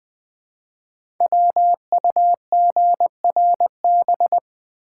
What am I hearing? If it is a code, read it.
Morse code, heard as a keyed tone, about 20 wpm, 700 Hz. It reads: WUGRB
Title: Code Groups - 5 Letters - 20wpm